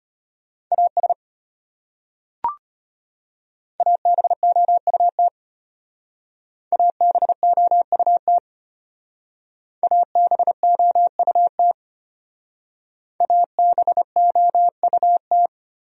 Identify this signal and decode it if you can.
Morse code, heard as a keyed tone, about 34 wpm, 700 Hz. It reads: AS ABOUT ABOUT ABOUT ABOUT